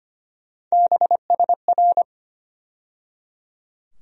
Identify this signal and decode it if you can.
Morse code, heard as a keyed tone, about 25 wpm, 700 Hz. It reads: BSR
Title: USA Call Sign Postfixes - 3 Letters 25wpm